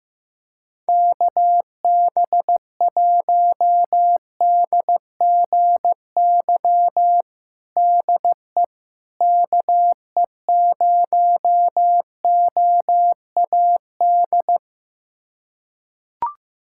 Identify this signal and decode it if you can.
Morse code, heard as a keyed tone, about 15 wpm, 700 Hz. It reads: KB1DGY DE KE0OAD
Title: USA Callsign DE Callsign - 6 Characters - 15wpm